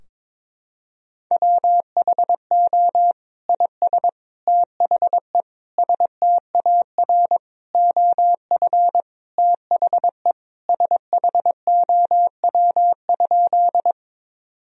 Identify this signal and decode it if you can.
Morse code, heard as a keyed tone, about 22 wpm, 700 Hz. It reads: WHO IS THE STAR OF THE SHOW?
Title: Sentences from Top 1000 Words - Review Part 2 - 22wpm